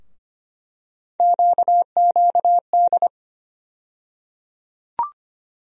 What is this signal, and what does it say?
Morse code, heard as a keyed tone, about 25 wpm, 700 Hz. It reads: QQD
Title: USA Call Sign Postfixes - 3 Letters 25wpm